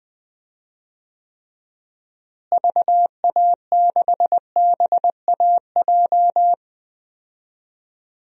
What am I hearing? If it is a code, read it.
Morse code, heard as a keyed tone, about 20 wpm, 700 Hz. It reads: VA6BAJ
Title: DXCC-100 Call Signs - 6 Characters - Repeated 3x - 20wpm